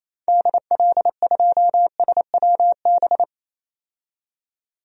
Morse code, heard as a keyed tone, about 28 wpm, 700 Hz. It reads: DL2SWB
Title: DXCC-100 Call Signs - 6 Characters - 28wpm